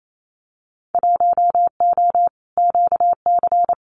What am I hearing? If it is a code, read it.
Morse code, heard as a keyed tone, about 28 wpm, 700 Hz. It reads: 1O QC
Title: Winter Field Day Exchanges 28wpm